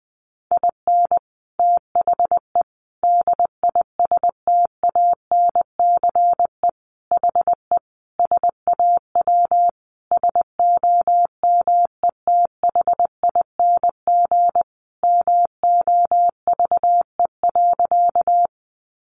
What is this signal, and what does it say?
Morse code, heard as a keyed tone, about 20 wpm, 700 Hz. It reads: IN THE DISTANCE HE SAW SOMETHING MOVE.
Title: Sentences from Top 1200 Words 20wpm